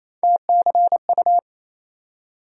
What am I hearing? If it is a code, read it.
Morse code, heard as a keyed tone, about 28 wpm, 700 Hz. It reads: TCU
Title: USA Call Sign Postfixes - 3 Letters 28wpm